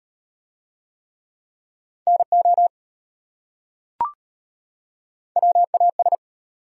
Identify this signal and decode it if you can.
Morse code, heard as a keyed tone, about 33 wpm, 700 Hz. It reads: NO WAS